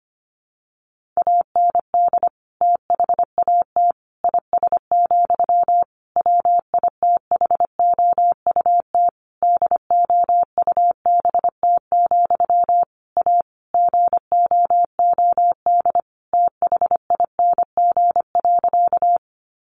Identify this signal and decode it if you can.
Morse code, heard as a keyed tone, about 25 wpm, 700 Hz. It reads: AND THAT IS, WITHOUT DOUBT, A GOOD THING.
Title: Sentences from Top 1500 Words 25wpm